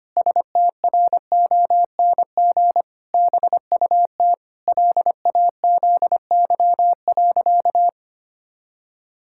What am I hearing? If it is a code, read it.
Morse code, heard as a keyed tone, about 25 wpm, 700 Hz. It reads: STRONG BUT LAZY.
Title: Pangram Challenge v2 - 25wpm